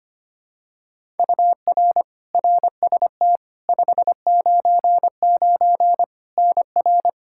Morse code, heard as a keyed tone, about 25 wpm, 700 Hz. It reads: UR RST 599 NR
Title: Practice QSO - Exchange 1 - Beginning - 25wpm